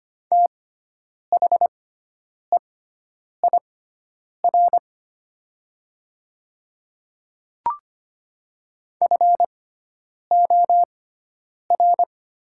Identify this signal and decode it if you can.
Morse code, heard as a keyed tone, about 25 wpm, spaced as for 9 wpm, 700 Hz. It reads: THEIR FOR
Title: 6x Character Spacing - Top 100 Words - 25wpm